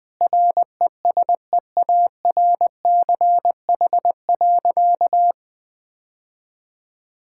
Morse code, heard as a keyed tone, about 20 wpm, 700 Hz. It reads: RESEARCH.
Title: Sentences from Top 1000 Words - Review Part 5 - 20wpm